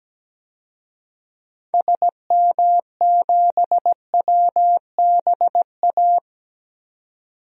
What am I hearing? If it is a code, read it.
Morse code, heard as a keyed tone, about 17 wpm, 700 Hz. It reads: SM7WBA
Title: DXCC-100 Call Signs - 6 Characters - 17wpm